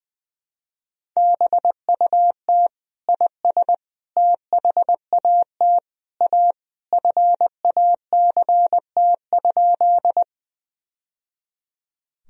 Morse code, heard as a keyed tone, about 20 wpm, 700 Hz. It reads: BUT IS THAT A FACT?
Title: Sentences from Top 1000 Words - Review Part 3 - 20wpm